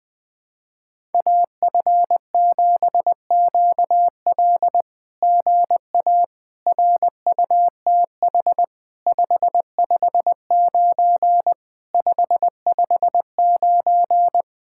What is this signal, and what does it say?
Morse code, heard as a keyed tone, about 20 wpm, 700 Hz. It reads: AF7QL GA RUTH 559 559
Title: SOTA QSO Practice 20wpm